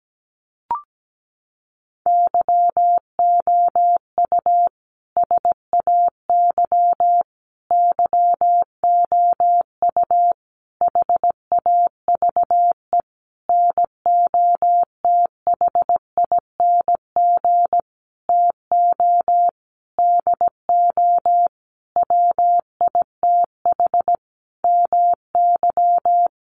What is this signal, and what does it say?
Morse code, heard as a keyed tone, about 17 wpm, 700 Hz. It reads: YOU SAY YOU HAVE NOTHING TO DO WITH MY
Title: Sentences from Top 700 Words 17wpm